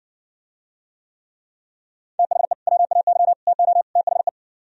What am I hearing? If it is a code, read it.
Morse code, heard as a keyed tone, about 59 wpm, 700 Hz. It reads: THE WAY TO THE